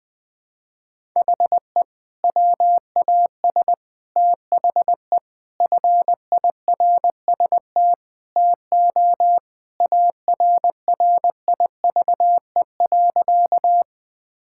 Morse code, heard as a keyed tone, about 20 wpm, 700 Hz. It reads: HE WAS THE FIRST TO ARRIVE.
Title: Sentences with 2-Word Phrases - 20wpm